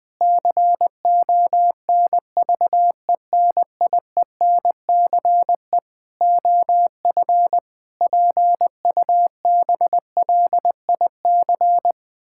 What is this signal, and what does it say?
Morse code, heard as a keyed tone, about 20 wpm, 700 Hz. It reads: CONVENIENCE OF PUBLIC
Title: Difficult Sentences with Hints - 20wpm